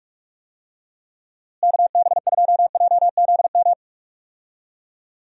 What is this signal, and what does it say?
Morse code, heard as a keyed tone, about 46 wpm, 700 Hz. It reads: KB2JZM
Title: USA Call Signs - 6 Characters - Repeated 3x - 45wpm